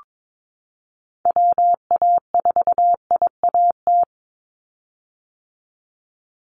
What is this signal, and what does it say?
Morse code, heard as a keyed tone, about 22 wpm, 700 Hz. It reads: WA4IAT
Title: USA Call Signs 22wpm